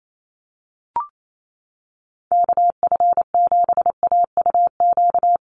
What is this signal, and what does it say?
Morse code, heard as a keyed tone, about 28 wpm, 700 Hz. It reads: KF7AUQ
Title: USA Call Signs 28wpm